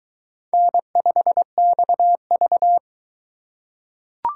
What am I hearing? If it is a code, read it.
Morse code, heard as a keyed tone, about 22 wpm, 700 Hz. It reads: N5XV